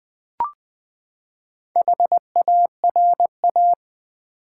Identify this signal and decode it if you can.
Morse code, heard as a keyed tone, about 20 wpm, 700 Hz. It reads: HARA
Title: Build A Word - 4 Letters At A Time - 10-Letter Words - 20wpm